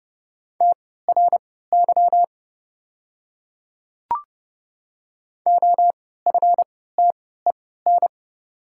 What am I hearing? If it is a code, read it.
Morse code, heard as a keyed tone, about 32 wpm, spaced as for 17 wpm, 700 Hz. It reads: TRY OFTEN
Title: 3x Character Spacing - Top 300 Words - 30wpm